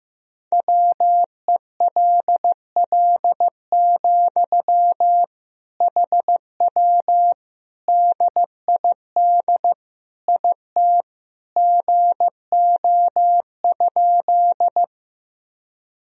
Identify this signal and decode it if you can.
Morse code, heard as a keyed tone, about 15 wpm, 700 Hz. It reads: WELL, HW DID IT GO?
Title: Sentences from Top 200 Words and Abbreviations 15wpm